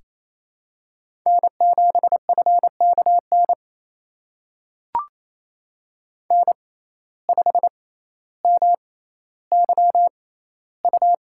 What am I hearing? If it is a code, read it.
Morse code, heard as a keyed tone, about 28 wpm, spaced as for 14 wpm, 700 Hz. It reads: N7FKN N5MYU
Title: USA Call Signs - 5 Characters - 6x Character Spacing - 28wpm